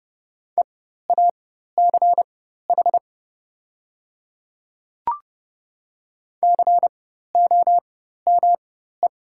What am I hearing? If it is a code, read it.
Morse code, heard as a keyed tone, about 30 wpm, spaced as for 14 wpm, 700 Hz. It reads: EACH COME
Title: 4x Character Spacing - Top 100 Words - 30wpm